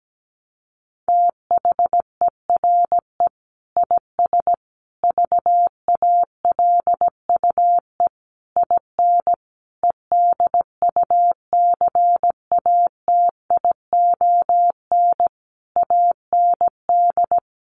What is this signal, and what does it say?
Morse code, heard as a keyed tone, about 17 wpm, 700 Hz. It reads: THERE IS VALUE IN EDUCATION AND
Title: Sentences from Top 1100 Words 17wpm